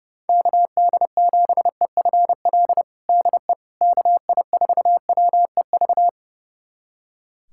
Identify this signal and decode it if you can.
Morse code, heard as a keyed tone, about 30 wpm, 700 Hz. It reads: KD7EFL DE KI4WEV
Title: USA Callsign DE Callsign - 6 Characters - 30wpm